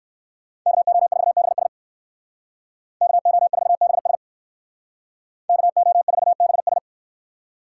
Morse code, heard as a keyed tone, about 69 wpm, 700 Hz. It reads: KO4BS KO4BS KO4BS